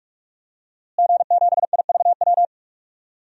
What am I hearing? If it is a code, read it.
Morse code, heard as a keyed tone, about 46 wpm, 700 Hz. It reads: G7IVW
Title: DXCC-100 Call Signs - 5 Characters - Repeated 3x - 45wpm